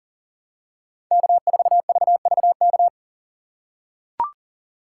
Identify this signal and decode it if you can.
Morse code, heard as a keyed tone, about 40 wpm, 700 Hz. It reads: K4VVK
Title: USA Call Signs - 5 Characters - 3x Character Spacing - 40wpm